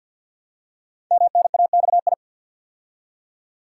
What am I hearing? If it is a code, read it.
Morse code, heard as a keyed tone, about 49 wpm, 700 Hz. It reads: MNAXI
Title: Code Groups - 5 Letters - 50wpm